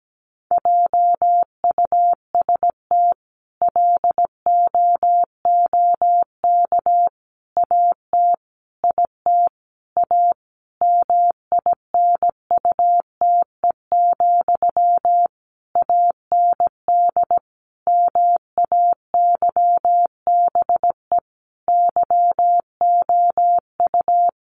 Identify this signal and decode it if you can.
Morse code, heard as a keyed tone, about 17 wpm, 700 Hz. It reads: JUST LOOK AT IT A MINUTE, AND MAYBE YOU
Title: Sentences from Top 1000 Words 17wpm